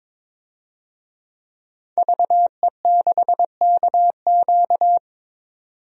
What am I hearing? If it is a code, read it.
Morse code, heard as a keyed tone, about 22 wpm, 700 Hz. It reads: VE6KQ
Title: DXCC-100 Call Signs - 5 Characters - 22wpm